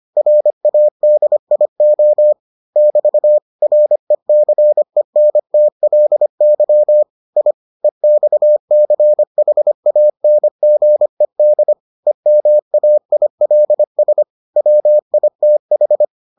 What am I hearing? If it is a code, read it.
Morse code, heard as a keyed tone, about 25 wpm, 600 Hz. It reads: RADIO <BT> RECENTLY I EXCHANGED EMAILS WITH